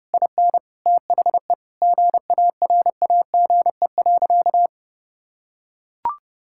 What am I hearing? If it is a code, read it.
Morse code, heard as a keyed tone, about 30 wpm, 700 Hz. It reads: IN THE GARAGE.
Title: Sentences from Top 1600 Words 30wpm